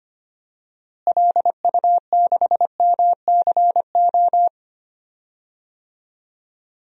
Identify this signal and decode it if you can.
Morse code, heard as a keyed tone, about 25 wpm, 700 Hz. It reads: LU6MCO
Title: DXCC-100 Call Signs - 6 Characters - Repeated 2x - 25wpm